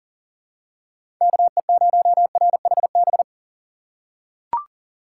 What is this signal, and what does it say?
Morse code, heard as a keyed tone, about 40 wpm, 700 Hz. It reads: KE0RHB
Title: USA Call Signs - 6 Characters - 2x Character Spacing - 40wpm